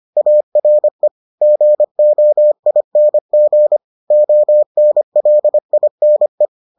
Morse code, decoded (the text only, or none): ARE GOING ONLINE